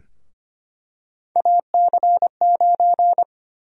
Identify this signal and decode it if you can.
Morse code, heard as a keyed tone, about 25 wpm, 700 Hz. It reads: AC9
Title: USA Call Sign Prefixes - Repeated 2x - 25wpm